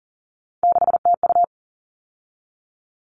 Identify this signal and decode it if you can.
Morse code, heard as a keyed tone, about 40 wpm, 700 Hz. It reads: 6TU
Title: All Cut Numbers - 3-Digit 40wpm